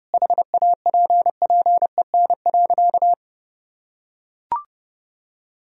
Morse code, heard as a keyed tone, about 30 wpm, 700 Hz. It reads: HAPPEN.
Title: Sentences from Top 1000 Words - Review Part 4 - 30wpm